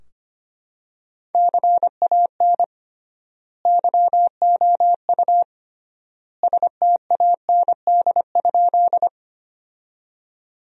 Morse code, heard as a keyed tone, about 25 wpm, 700 Hz. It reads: CAN YOU STAND?
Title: 3x Word-Spacing - Sentences from Top 300 Words - 25wpm